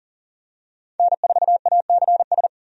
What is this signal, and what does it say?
Morse code, heard as a keyed tone, about 40 wpm, 700 Hz. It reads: N4ACS
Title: USA Call Signs - 5 Characters - 40wpm